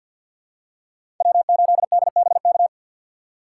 Morse code, heard as a keyed tone, about 52 wpm, 700 Hz. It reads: W8DBK
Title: USA Call Signs - 6x Character Spacing - 50wpm